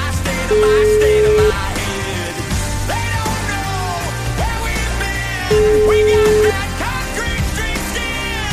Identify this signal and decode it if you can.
Ringback tone (Europe, 425 Hz), with noise.